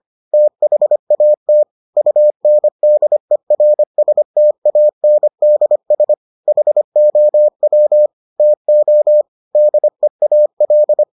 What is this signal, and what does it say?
Morse code, heard as a keyed tone, about 25 wpm, 600 Hz. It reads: THAT UNDERSTANDS HOW TO DEAL